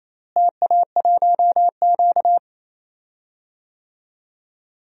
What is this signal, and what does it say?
Morse code, heard as a keyed tone, about 28 wpm, 700 Hz. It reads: TA1Q
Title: DXCC-100 Call Signs - 4 Characters - Repeated 2x - 28wpm